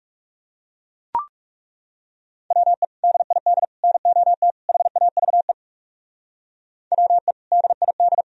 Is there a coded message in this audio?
Yes: Morse code from 2.5 to 8.2 s, heard as a keyed tone, about 39 wpm, 700 Hz. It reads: WE DID NOT HAVE WE DID